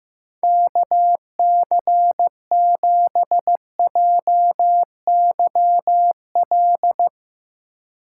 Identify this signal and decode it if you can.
Morse code, heard as a keyed tone, about 15 wpm, 700 Hz. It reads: KC7JYL
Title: Speed Racing - USA Call Signs 15wpm